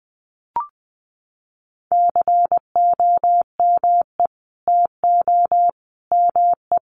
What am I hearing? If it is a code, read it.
Morse code, heard as a keyed tone, about 20 wpm, 700 Hz. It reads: COME TO ME
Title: Sentences from Top 900 Words 20wpm